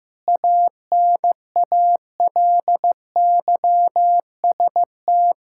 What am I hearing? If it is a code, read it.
Morse code, heard as a keyed tone, about 15 wpm, 700 Hz. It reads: ANALYST
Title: Practice QSO - Exchange 3 - More Info - 15wpm